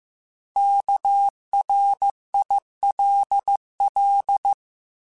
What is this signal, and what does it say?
Morse code, heard as a keyed tone, about 15 wpm, 800 Hz. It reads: KRILL
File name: Alphabetical / Alphabetical mp3s / Ks @13